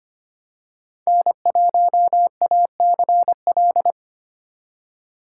Morse code, heard as a keyed tone, about 25 wpm, 700 Hz. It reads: N1ACL
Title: USA Call Signs - 4x Character Spacing - 25wpm